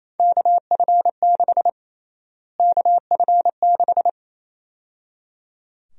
Morse code, heard as a keyed tone, about 27 wpm, 700 Hz. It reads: KF6 KF6
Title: USA Call Sign Prefixes - Repeated 2x - 28wpm